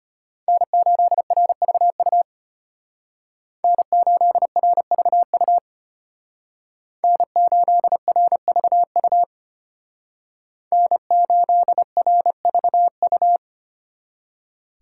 Morse code, heard as a keyed tone, about 31 wpm, 700 Hz. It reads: N8RVU N8RVU N8RVU N8RVU
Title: Speed Racing - USA 1x3 Call Signs 25wpm